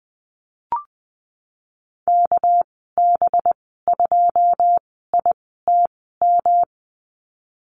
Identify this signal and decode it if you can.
Morse code, heard as a keyed tone, about 20 wpm, spaced as for 15 wpm, 700 Hz. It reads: KB2ITM
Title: 200 Lesson 32g - Letter k - Call Signs 20wpm